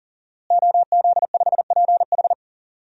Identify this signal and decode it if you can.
Morse code, heard as a keyed tone, about 40 wpm, 700 Hz. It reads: OZ5PH
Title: DXCC-100 Call Signs - 5 Characters - Repeated 3x - 40wpm